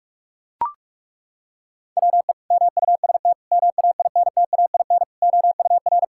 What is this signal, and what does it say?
Morse code, heard as a keyed tone, about 41 wpm, 700 Hz. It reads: WE MUST MAINTAIN OUR
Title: Sentences from Top 1000 Words - Review Part 5 - 45wpm